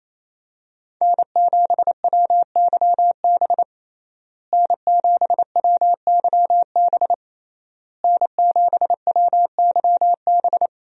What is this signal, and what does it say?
Morse code, heard as a keyed tone, about 28 wpm, 700 Hz. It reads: N7WYB N7WYB N7WYB
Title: USA Call Signs - 5 Characters - Repeated 3x - 28wpm